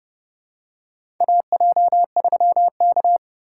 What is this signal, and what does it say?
Morse code, heard as a keyed tone, about 30 wpm, 700 Hz. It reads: AJ3K
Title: USA Call Signs - 4 Characters - 6x Character Spacing - 30wpm